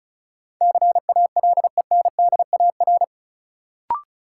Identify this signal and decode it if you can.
Morse code, heard as a keyed tone, about 36 wpm, 700 Hz. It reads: CALENDAR
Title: Sentences from Top 1600 Words - New Word Drill Alt - 35wpm